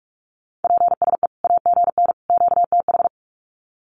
Morse code, heard as a keyed tone, about 45 wpm, 700 Hz. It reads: PSE AGN QTH
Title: Common QSO Phrases 45wpm